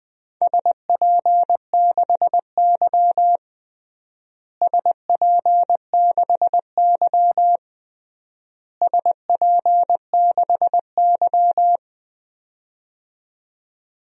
Morse code, heard as a keyed tone, about 21 wpm, 700 Hz. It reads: SP6Y SP6Y SP6Y
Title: DXCC-100 Call Signs - 4 Characters - Repeated 3x - 20wpm